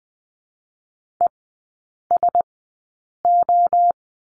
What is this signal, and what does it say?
Morse code, heard as a keyed tone, about 20 wpm, spaced as for 8 wpm, 700 Hz. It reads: ESO
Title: 041 Lesson 8e - Number 1 - Three Characters 20wpm